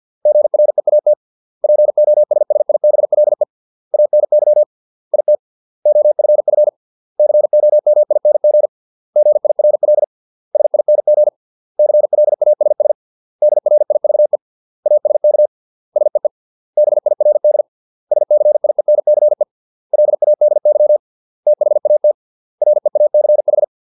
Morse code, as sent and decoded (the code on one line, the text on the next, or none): --. .-. .- - / .--. --- ... ... .. -... .-.. / .- -. -.-- / .. - / --- ..-. ..-. / -.-. --- -- .. -. --. / --. .. .-. .-.. / ... .. -. --. / -.-. .-.. .- ... ... / -.. .-. .. ...- . / .- ... -.- / ... . / -... .. .-. -.. / ... -.-. .. -. -.-. / .-.. .- -.. -.-- / - .... .- - / .-. .- -.-. ....
GREAT POSSIBLE ANY IT OFF COMING GIRL SING CLASS DRIVE ASK SEE BIRD SCIENCE LADY THAT REACH